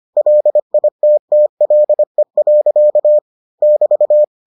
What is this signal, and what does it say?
Morse code, heard as a keyed tone, about 25 wpm, 600 Hz. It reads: LITTLE. <BT>